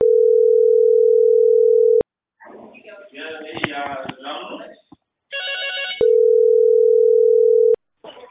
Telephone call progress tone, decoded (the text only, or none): ringback tone